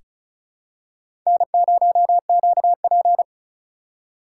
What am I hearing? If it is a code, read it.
Morse code, heard as a keyed tone, about 34 wpm, 690 Hz. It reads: N0QP
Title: USA Call Signs - 4 Characters - 2x Character Spacing - 35wpm